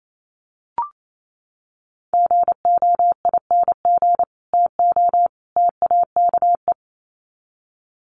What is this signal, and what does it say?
Morse code, heard as a keyed tone, about 28 wpm, 700 Hz. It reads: GOING TO TAKE
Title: Sets of 3 Words 28wpm